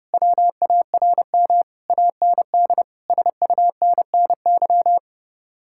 Morse code, heard as a keyed tone, about 28 wpm, 700 Hz. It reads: WARM AND SUNNY
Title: Speed-Racing Common QSO Phrases Encore - 30wpm